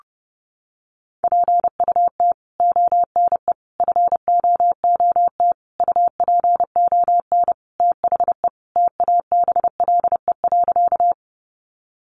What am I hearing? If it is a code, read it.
Morse code, heard as a keyed tone, about 30 wpm, 700 Hz. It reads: PUT ONE FOOT UPON THE TABLE.
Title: Sentences from Top 1100 Words 30wpm